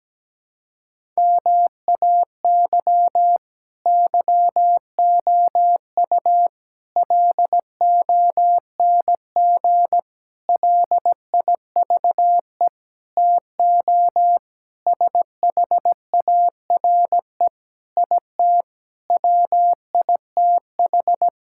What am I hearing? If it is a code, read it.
Morse code, heard as a keyed tone, about 17 wpm, 700 Hz. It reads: MAY YOU LONG LIVE TO SHARE IT WITH
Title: Sentences from Top 1000 Words - Review Part 6 - 17wpm